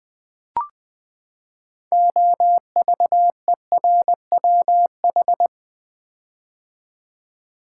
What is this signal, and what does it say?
Morse code, heard as a keyed tone, about 20 wpm, 700 Hz. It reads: OVERWH
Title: Build A Word - 6 Letters At A Time - 12-Letter Words - 20wpm